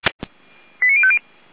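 Five-tone selective call (ZVEI), heard as quick runs of five tones, tones of about 70 ms. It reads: 89050
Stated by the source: Five-tone ZVEI selective call received by SDR.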